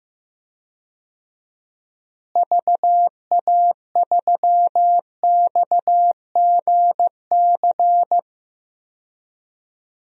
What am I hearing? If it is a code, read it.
Morse code, heard as a keyed tone, about 15 wpm, 700 Hz. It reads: VA3XGC
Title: DXCC-100 Call Signs - 6 Characters - Repeated 3x - 15wpm